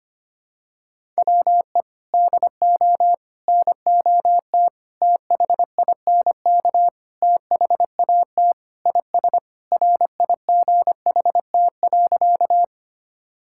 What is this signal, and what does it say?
Morse code, heard as a keyed tone, about 25 wpm, 700 Hz. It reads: WE DO NOT THINK THAT IS RIGHT.
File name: practice / 25wpm / Sentences with 4-Word Phrases - 25wpm